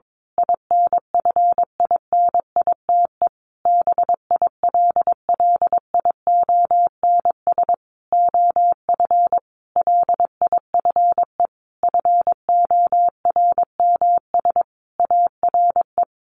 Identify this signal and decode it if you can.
Morse code, heard as a keyed tone, about 22 wpm, 700 Hz. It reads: INFINITE BILLIONS OF LIFE FORMS ARE
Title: A Sound Of Thunder 22wpm